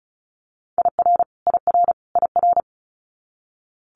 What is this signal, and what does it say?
Morse code, heard as a keyed tone, about 36 wpm, 700 Hz. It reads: IR IR IR
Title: Multi-letter Phonemes Repeated 3x 35wpm